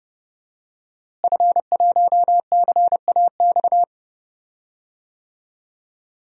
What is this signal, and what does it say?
Morse code, heard as a keyed tone, about 30 wpm, 700 Hz. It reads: F1CAX
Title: DXCC-100 Call Signs - 5 Characters - Repeated 2x - 30wpm